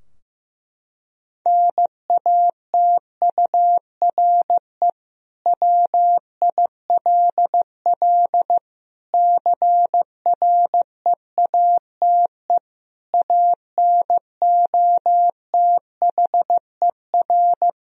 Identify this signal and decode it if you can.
Morse code, heard as a keyed tone, about 15 wpm, 700 Hz. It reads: NATURE WILL CREATE ANOTHER
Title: Sentences from Top 1000 Words - Review Part 3 - 15wpm